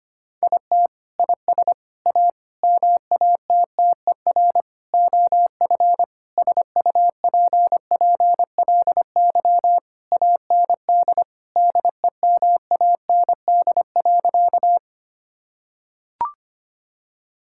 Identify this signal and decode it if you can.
Morse code, heard as a keyed tone, about 25 wpm, 700 Hz. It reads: IT IS A MATTER OF SUPPLY AND DEMAND.
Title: Sentences from Top 1000 Words - Review Part 5 - 25wpm